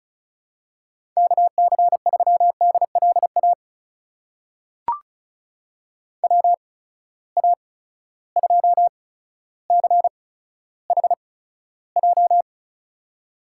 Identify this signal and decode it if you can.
Morse code, heard as a keyed tone, about 34 wpm, spaced as for 15 wpm, 700 Hz. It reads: KC3DLA WA2CHJ
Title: USA Call Signs - 6 Characters - 8x Character Spacing - 35wpm